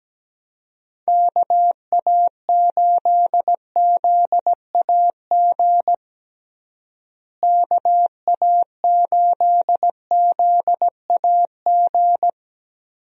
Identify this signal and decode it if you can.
Morse code, heard as a keyed tone, about 17 wpm, 700 Hz. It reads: KA8ZAG KA8ZAG
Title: USA Call Signs - 6 Characters - Repeated 2x - 17wpm